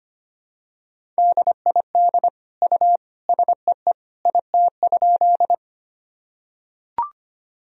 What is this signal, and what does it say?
Morse code, heard as a keyed tone, about 25 wpm, 700 Hz. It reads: DID U SEE IT?
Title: Sentences from Top 100 Words and Abbreviations 25wpm